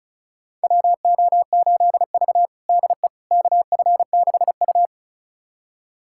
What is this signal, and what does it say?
Morse code, heard as a keyed tone, about 36 wpm, 700 Hz. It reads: WO8V DE KF6U
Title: USA Callsign DE Callsign - 4 Characters - 35wpm